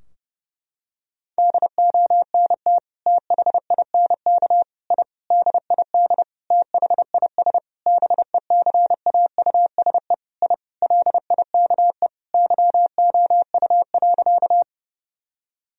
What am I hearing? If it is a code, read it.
Morse code, heard as a keyed tone, about 30 wpm, 700 Hz. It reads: DONT THINK I DID THIS BECAUSE I LIKE YOU.
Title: Sentences from Top 1000 Words - Review Part 3 - 30wpm